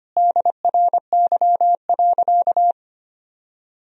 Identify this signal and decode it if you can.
Morse code, heard as a keyed tone, about 25 wpm, 700 Hz. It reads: DRY.
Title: Sentences from Top 1000 Words - Review Part 4 - 25wpm